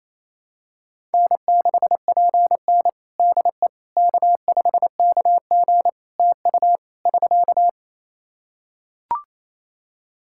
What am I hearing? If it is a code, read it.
Morse code, heard as a keyed tone, about 27 wpm, 700 Hz. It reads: N6PN DE K5KG TU <SK>
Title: Practice QSO - Exchange 4 - Ending - 28wpm